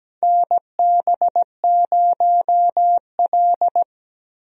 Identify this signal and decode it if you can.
Morse code, heard as a keyed tone, about 17 wpm, 700 Hz. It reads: NB0L
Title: USA Call Signs - 4 Characters - Repeated 2x - 17wpm